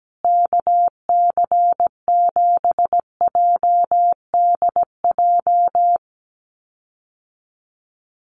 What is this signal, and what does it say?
Morse code, heard as a keyed tone, about 17 wpm, 700 Hz. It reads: KC7JDJ
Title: USA Call Signs Repeated 2x 17wpm